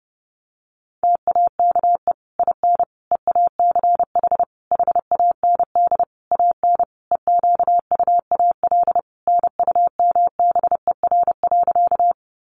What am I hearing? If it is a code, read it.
Morse code, heard as a keyed tone, about 30 wpm, 700 Hz. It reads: TAKE IN EACH HAND AN EQUAL NUMBER.
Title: Sentences from Top 1400 Words 30wpm